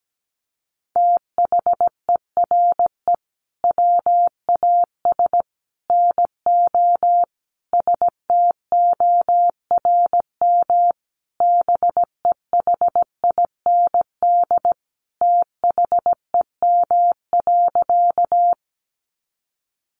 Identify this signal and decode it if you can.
Morse code, heard as a keyed tone, about 17 wpm, 700 Hz. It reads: THERE WAS NO STORM BEHIND THEM.
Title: Sentences from Top 1500 Words 17wpm